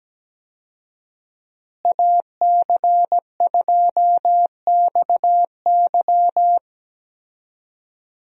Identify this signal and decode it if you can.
Morse code, heard as a keyed tone, about 17 wpm, 700 Hz. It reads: AC2XY
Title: USA Call Signs - 5 Characters - Repeated 2x - 17wpm